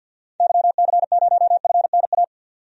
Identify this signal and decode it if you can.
Morse code, heard as a keyed tone, about 51 wpm, 700 Hz. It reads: YC0FNA
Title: DXCC-100 Call Signs - 6 Characters - 50wpm